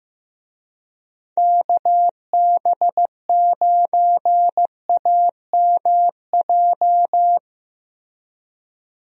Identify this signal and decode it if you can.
Morse code, heard as a keyed tone, about 15 wpm, 700 Hz. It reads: KB9AMJ
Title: USA Call Signs - 6 Characters - Repeated 3x - 15wpm